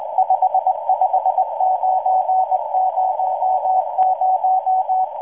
Morse code, heard as a keyed tone, about 19 wpm, 750 Hz, with noise. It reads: 55900